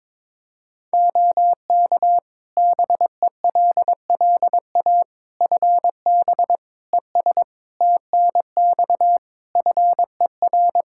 Morse code, heard as a keyed tone, about 22 wpm, 700 Hz. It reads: OK BELLA FB ES TNX FER
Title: K1USN SST QSO Practice 22wpm